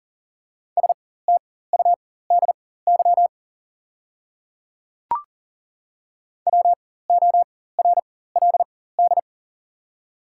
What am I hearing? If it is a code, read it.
Morse code, heard as a keyed tone, about 40 wpm, spaced as for 19 wpm, 700 Hz. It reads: STUDY WORLD
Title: 4x Character Spacing - Top 300 Words - 40wpm